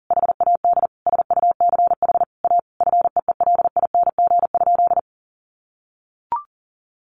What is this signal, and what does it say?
Morse code, heard as a keyed tone, about 40 wpm, 700 Hz. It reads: HAD SUCH A FEELING?
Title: Sentences from Top 1200 Words 40wpm